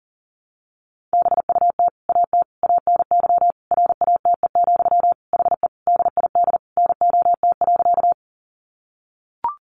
Morse code, heard as a keyed tone, about 40 wpm, 700 Hz. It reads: BUT AT ANY RATE, HE DID NOT.
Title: Sentences from Top 700 Words 40wpm